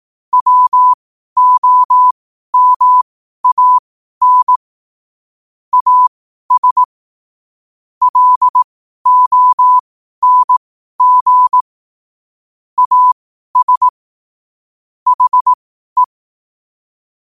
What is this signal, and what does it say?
Morse code, heard as a keyed tone, about 18 wpm, spaced as for 12 wpm, 1000 Hz. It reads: WOMAN AS LONG AS HE
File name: qotdfiles / 12wpm